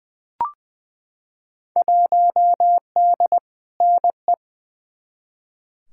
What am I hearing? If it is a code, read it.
Morse code, heard as a keyed tone, about 20 wpm, 700 Hz. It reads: 1D NE
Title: Field Day Exchanges - 20wpm